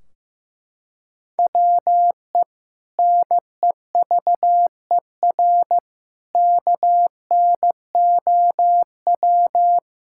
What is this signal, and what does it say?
Morse code, heard as a keyed tone, about 15 wpm, 700 Hz. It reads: WE NEVER KNOW
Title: Sentences from Top 1600 Words 15wpm